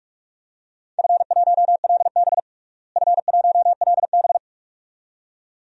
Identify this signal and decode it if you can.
Morse code, heard as a keyed tone, about 42 wpm, 700 Hz. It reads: F1LB F1LB
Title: DXCC-100 Call Signs - 4 Characters - Repeated 2x - 45wpm